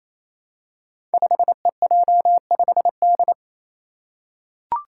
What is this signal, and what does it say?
Morse code, heard as a keyed tone, about 27 wpm, 700 Hz. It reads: 5EJ5D
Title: Five Challenging Letter-Number Combinations 28wpm